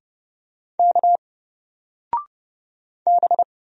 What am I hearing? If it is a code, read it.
Morse code, heard as a keyed tone, about 30 wpm, 700 Hz. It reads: K B
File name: Single-Letter-Number-30wpm